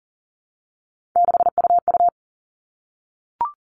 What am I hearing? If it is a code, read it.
Morse code, heard as a keyed tone, about 40 wpm, 700 Hz. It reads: 6UU
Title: All Cut Numbers - 3-Digit 40wpm